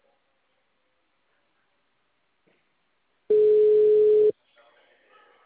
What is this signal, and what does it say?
Ringback tone (Europe, 425 Hz).
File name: out- -2023- - - .46421